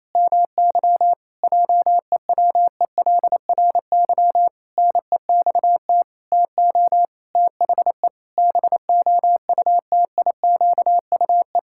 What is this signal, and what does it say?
Morse code, heard as a keyed tone, about 28 wpm, 700 Hz. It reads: MY JEWELRY NEXT TO THE BOUTIQUE
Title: Pangram Challenge v2 - 28wpm